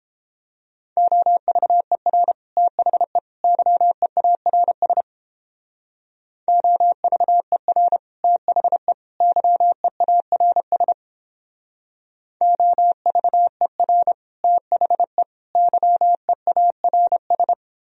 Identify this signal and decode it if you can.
Morse code, heard as a keyed tone, about 31 wpm, 700 Hz. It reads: OVER THE YEARS OVER THE YEARS OVER THE YEARS